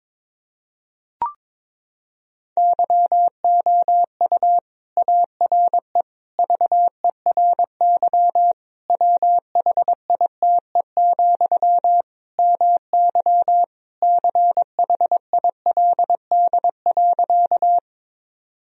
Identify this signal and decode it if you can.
Morse code, heard as a keyed tone, about 22 wpm, 700 Hz. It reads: YOU ARE VERY WHITE, MY CHILD.
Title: Sentences from Top 1000 Words - Review Part 1 - 22wpm